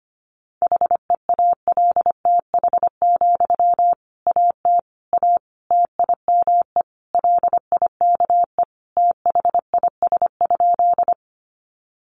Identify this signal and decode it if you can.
Morse code, heard as a keyed tone, about 25 wpm, 700 Hz. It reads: HEALTH, AT A TIME LIKE THIS?
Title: Sentences from Top 600 Words 25wpm